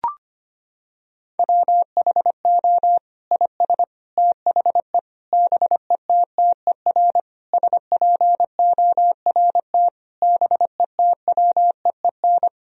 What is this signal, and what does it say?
Morse code, heard as a keyed tone, about 25 wpm, 700 Hz. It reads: WHO IS THE BETTER SPORT BETWEEN
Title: Sentences from Top 1000 Words - Review Part 4 - 25wpm